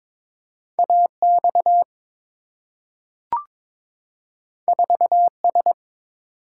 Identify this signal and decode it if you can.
Morse code, heard as a keyed tone, about 22 wpm, 700 Hz. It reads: AX 4S
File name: practice / 22wpm / Two Letter-Number Combinations - 22wpm